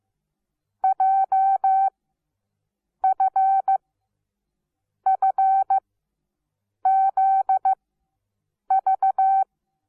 Morse code, heard as a keyed tone, about 15 wpm, spaced as for 6 wpm, 770 Hz. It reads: JFFZV
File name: Onlinelehrgang / morsen / DJ4UF Morsen17